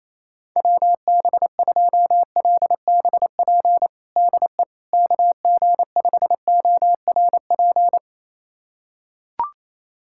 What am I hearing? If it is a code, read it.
Morse code, heard as a keyed tone, about 28 wpm, 700 Hz. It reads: WB2LBP DE KG5ORP
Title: USA Callsign DE Callsign - 6 Characters - 28wpm